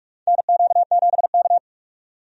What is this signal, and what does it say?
Morse code, heard as a keyed tone, about 43 wpm, 700 Hz. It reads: NQ7K
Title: USA Call Signs - 4 Characters - 45wpm